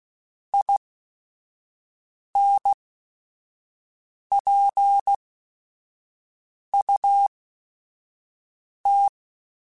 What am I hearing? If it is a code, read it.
Morse code, heard as a keyed tone, about 16 wpm, spaced as for 5 wpm, 800 Hz. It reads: INPUT
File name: Technical Subjects / Technical Subjects-mp3s / OR Gate @5